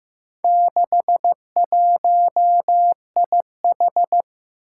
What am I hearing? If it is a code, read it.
Morse code, heard as a keyed tone, about 15 wpm, 700 Hz. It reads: 61IH
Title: Four Challenging Letter-Number Combinations 15wpm